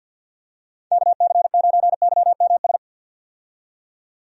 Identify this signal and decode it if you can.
Morse code, heard as a keyed tone, about 50 wpm, 700 Hz. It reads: KC9YMS
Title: USA Call Signs - 6 Characters - 50wpm